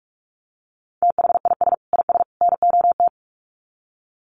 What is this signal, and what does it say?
Morse code, heard as a keyed tone, about 44 wpm, 700 Hz. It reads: THIS IS NOT
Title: Sets of 3 Words 45wpm